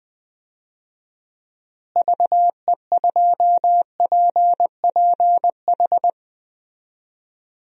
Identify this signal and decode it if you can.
Morse code, heard as a keyed tone, about 21 wpm, 700 Hz. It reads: VE2PPH
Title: DXCC-100 Call Signs - 6 Characters - Repeated 3x - 20wpm